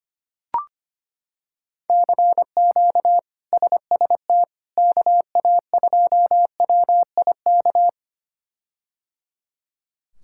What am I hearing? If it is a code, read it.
Morse code, heard as a keyed tone, about 25 wpm, 700 Hz. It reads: CQ SST KA2WIK
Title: K1USN SST QSO Practice - 25wpm